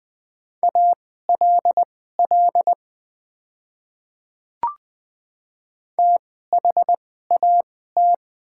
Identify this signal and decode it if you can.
Morse code, heard as a keyed tone, about 20 wpm, spaced as for 14 wpm, 700 Hz. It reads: ALL THAT
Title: 2x Character Spacing - Top 100 Words - 20wpm